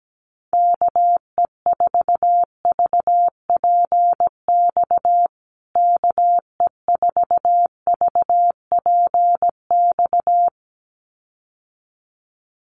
Morse code, heard as a keyed tone, about 17 wpm, 700 Hz. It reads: KE4VPX KE4VPX
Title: USA Call Signs Repeated 2x 17wpm